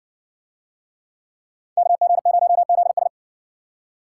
Morse code, heard as a keyed tone, about 60 wpm, 700 Hz. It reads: KG0ZS